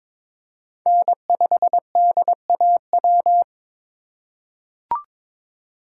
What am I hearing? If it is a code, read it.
Morse code, heard as a keyed tone, about 21 wpm, 700 Hz. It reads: N5DAW
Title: USA Call Signs - 5 Characters - 2x Character Spacing - 22wpm